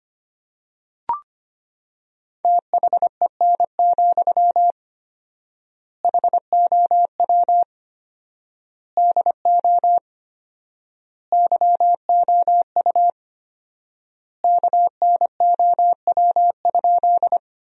Morse code, heard as a keyed tone, about 25 wpm, 700 Hz. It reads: THEN, HOW DO YOU KNOW?
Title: 4x Word-Spacing - Sentences from Top 100 Words - 25wpm